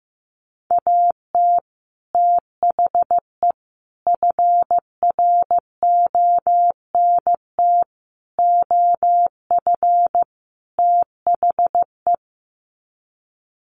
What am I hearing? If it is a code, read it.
Morse code, heard as a keyed tone, about 15 wpm, 700 Hz. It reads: AT THE FRONT OF THE
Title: Sets of 5 Words 15wpm